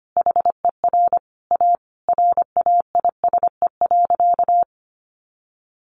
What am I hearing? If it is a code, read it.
Morse code, heard as a keyed tone, about 25 wpm, 700 Hz. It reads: HER A RAISE.
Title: Easy Phrases 25wpm